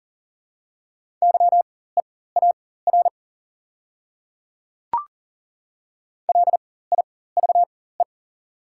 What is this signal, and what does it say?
Morse code, heard as a keyed tone, about 38 wpm, spaced as for 18 wpm, 700 Hz. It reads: YEAR LIVE
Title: 4x Character Spacing - Top 200 Words - 40wpm